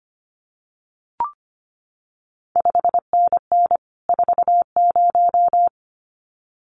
Morse code, heard as a keyed tone, about 25 wpm, 700 Hz. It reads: 5NN 40
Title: CQ WW DX Contest Exchange Encore - 25wpm